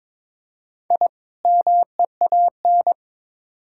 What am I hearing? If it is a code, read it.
Morse code, heard as a keyed tone, about 22 wpm, 700 Hz. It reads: I MEAN
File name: Build A Phrase - 2-Word Phrases - 22wpm